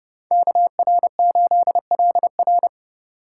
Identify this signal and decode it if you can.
Morse code, heard as a keyed tone, about 30 wpm, 700 Hz. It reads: KR8LR
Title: Speed Racing - USA 2x2 Call Signs 25wpm